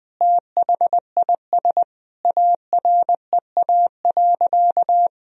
Morse code, heard as a keyed tone, about 20 wpm, 700 Hz. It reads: THIS AREA.
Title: Sentences from Top 1000 Words - Review Part 2 - 20wpm